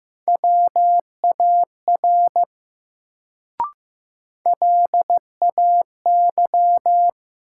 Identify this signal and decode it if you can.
Morse code, heard as a keyed tone, about 15 wpm, 700 Hz. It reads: WAR LAY
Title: Sentences from Top 400 Words - New Word Drill - 15wpm